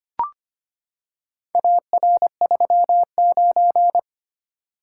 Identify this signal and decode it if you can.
Morse code, heard as a keyed tone, about 25 wpm, 700 Hz. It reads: AR39
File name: practice / 25wpm / Four Letter-Number Combinations - 25wpm